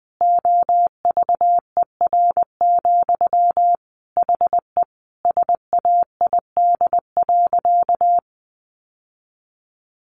Morse code, heard as a keyed tone, about 20 wpm, 700 Hz. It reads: OVER, HE SAID.
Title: Sentences from Top 600 Words 20wpm